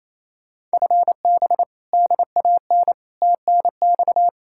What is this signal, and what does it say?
Morse code, heard as a keyed tone, about 37 wpm, 700 Hz. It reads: FB DAN TNX